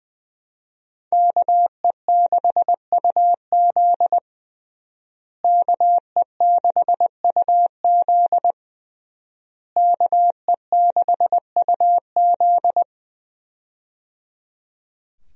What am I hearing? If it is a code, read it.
Morse code, heard as a keyed tone, about 20 wpm, 700 Hz. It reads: KE6UZ KE6UZ KE6UZ
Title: USA Call Signs - 5 Characters - Repeated 3x - 20wpm